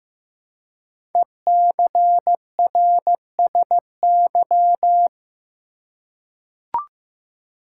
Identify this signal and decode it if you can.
Morse code, heard as a keyed tone, about 15 wpm, 700 Hz. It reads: ECRSY
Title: Code Groups - 5 Letters - 15wpm